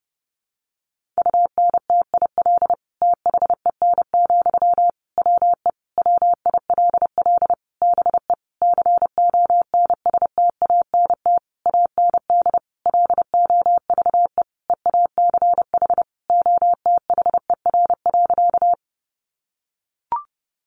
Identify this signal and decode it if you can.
Morse code, heard as a keyed tone, about 30 wpm, 700 Hz. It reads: UNTIL THEN, WE WILL BE CONSTANT AND LOVE EACH OTHER.
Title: Sentences from Top 1500 Words 30wpm